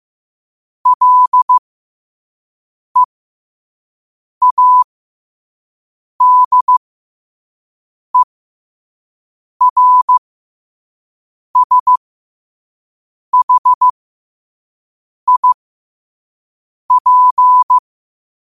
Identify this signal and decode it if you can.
Morse code, heard as a keyed tone, about 15 wpm, spaced as for 5 wpm, 1000 Hz. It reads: LEADERSHIP